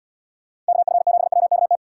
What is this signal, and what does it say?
Morse code, heard as a keyed tone, about 70 wpm, 700 Hz. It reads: KF7WKT